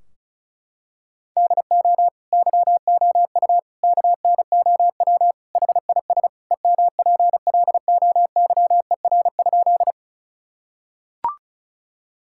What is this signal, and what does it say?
Morse code, heard as a keyed tone, about 34 wpm, 700 Hz. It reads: DO YOU KNOW HIS EMPLOYER?
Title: Sentences from Top 1600 Words 35wpm